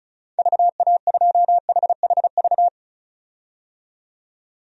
Morse code, heard as a keyed tone, about 34 wpm, 700 Hz. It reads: VA2HHV
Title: DXCC-100 Call Signs - 6 Characters - 35wpm